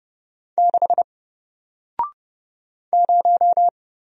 Morse code, heard as a keyed tone, about 30 wpm, 700 Hz. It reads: 6 0
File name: Single-Letter-Number-30wpm